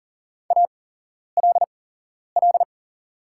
Morse code, heard as a keyed tone, about 40 wpm, spaced as for 11 wpm, 690 Hz. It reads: ALL
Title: 8x Character Spacing - Top 100 Words - 40wpm